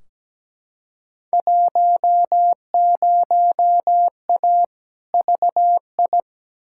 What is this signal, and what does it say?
Morse code, heard as a keyed tone, about 17 wpm, 700 Hz. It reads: 10A VI
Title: Field Day Exchanges Encore - 17wpm